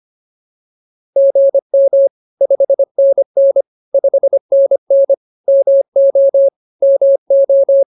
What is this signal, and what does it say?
Morse code, heard as a keyed tone, about 24 wpm, 550 Hz. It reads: GM 5NN 5NN MO MO